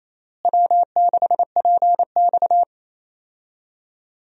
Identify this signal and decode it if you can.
Morse code, heard as a keyed tone, about 28 wpm, 700 Hz. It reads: W6PX
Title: USA Call Signs - 4 Characters - Repeated 3x - 28wpm